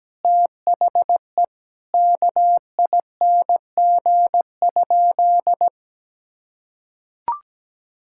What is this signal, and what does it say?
Morse code, heard as a keyed tone, about 17 wpm, 700 Hz. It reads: THE KING?
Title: Sentences from Top 1000 Words - Review Part 6 - 17wpm